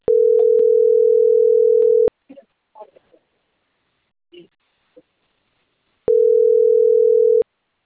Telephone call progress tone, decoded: ringback tone